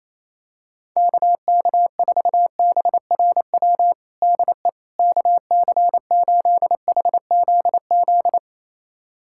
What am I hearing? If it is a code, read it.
Morse code, heard as a keyed tone, about 28 wpm, 700 Hz. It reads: KK4BRW DE KC8HZZ
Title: USA Callsign DE Callsign - 6 Characters - 28wpm